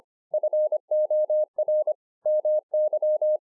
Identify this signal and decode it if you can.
Morse code, heard as a keyed tone, about 25 wpm, 610 Hz. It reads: FOR MY